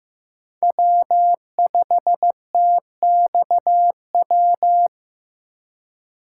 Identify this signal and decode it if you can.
Morse code, heard as a keyed tone, about 16 wpm, 700 Hz. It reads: W5TXW
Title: USA Call Signs - 5 Characters - 3x Character Spacing - 15wpm